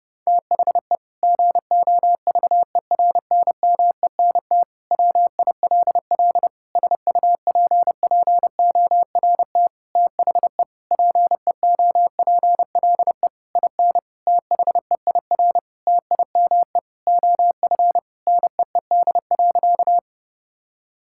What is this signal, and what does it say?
Morse code, heard as a keyed tone, about 30 wpm, 700 Hz. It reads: THE GOVERNMENT WILL SUPPORT THE PEOPLE IN THEIR TIME OF NEED.
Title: Sentences from Top 1000 Words - Review Part 5 - 30wpm